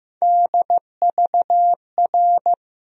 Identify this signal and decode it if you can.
Morse code, heard as a keyed tone, about 15 wpm, 700 Hz. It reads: DVR
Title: USA Call Sign Postfixes - 3 Letters 15wpm